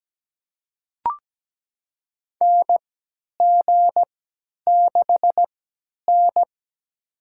Morse code, heard as a keyed tone, about 17 wpm, spaced as for 10 wpm, 700 Hz. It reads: NG6N
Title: USA Call Signs - 4 Characters - 3x Character Spacing - 17wpm